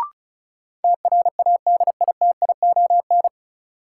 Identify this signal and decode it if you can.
Morse code, heard as a keyed tone, about 36 wpm, 700 Hz. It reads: TRADITION
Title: Sentences from Top 1500 Words - New Word Drill Alt - 35wpm